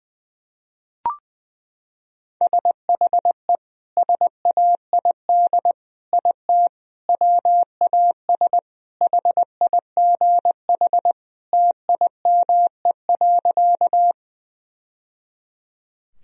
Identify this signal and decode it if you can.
Morse code, heard as a keyed tone, about 20 wpm, 700 Hz. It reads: SHE SAID IT WAS HIGH TIME.
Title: Sentences from Top 1000 Words - Review Part 1 - 20wpm